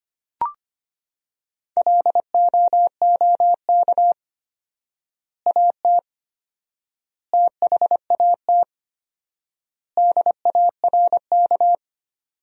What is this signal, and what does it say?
Morse code, heard as a keyed tone, about 25 wpm, 700 Hz. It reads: LOOK AT THAT DARK
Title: 4x Word-Spacing - Sentences from Top 500 Words - 25wpm